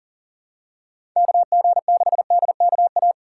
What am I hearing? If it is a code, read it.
Morse code, heard as a keyed tone, about 40 wpm, 700 Hz. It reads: KG6DKA
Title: USA Call Signs - 6 Characters - 6x Character Spacing - 40wpm